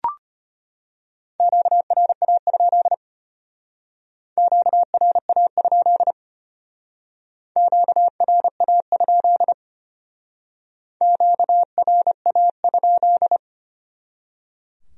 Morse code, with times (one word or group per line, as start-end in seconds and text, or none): 1.4-2.9 QRA?
4.4-6.1 QRA?
7.6-9.5 QRA?
11.0-13.4 QRA?